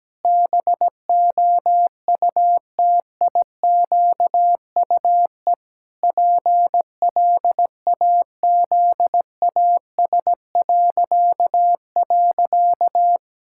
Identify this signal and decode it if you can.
Morse code, heard as a keyed tone, about 17 wpm, 700 Hz. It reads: BOUTIQUE PLAZAS..
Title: Pangram Challenge v2 - 17wpm